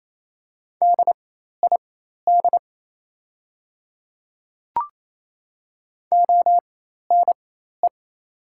Morse code, heard as a keyed tone, about 28 wpm, spaced as for 13 wpm, 700 Hz. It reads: DID ONE
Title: 4x Character Spacing - Top 100 Words - 28wpm